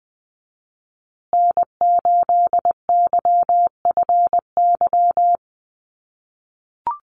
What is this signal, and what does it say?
Morse code, heard as a keyed tone, about 20 wpm, 700 Hz. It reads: N8YFY
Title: USA Call Signs Repeated 2x 20wpm